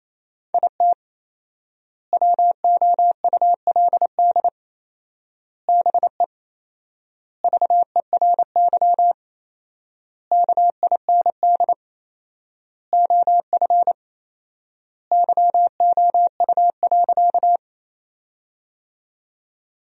Morse code, heard as a keyed tone, about 28 wpm, 700 Hz. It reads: IT WOULD BE VERY KIND OF YOU.
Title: 4x Word-Spacing - Sentences from Top 200 Words - 28wpm